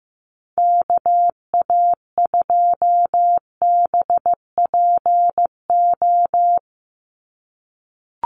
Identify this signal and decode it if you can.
Morse code, heard as a keyed tone, about 15 wpm, 700 Hz. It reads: KA2BPO
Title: USA Call Signs 15wpm